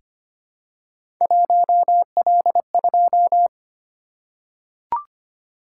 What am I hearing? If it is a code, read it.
Morse code, heard as a keyed tone, about 25 wpm, 700 Hz. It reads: 1L2
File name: Three Letter-Number Combinations - 25wpm